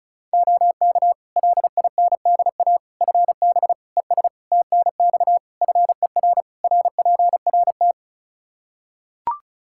Morse code, heard as a keyed tone, about 34 wpm, 700 Hz. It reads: OK LINDA FB ES TNX FER RPRT
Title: Practice QSO - Exchange 2 - Info - 35wpm